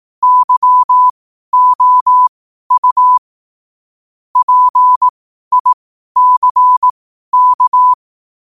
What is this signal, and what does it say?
Morse code, heard as a keyed tone, about 18 wpm, spaced as for 12 wpm, 1000 Hz. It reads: YOU PICK
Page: 12 wpm morse code quotes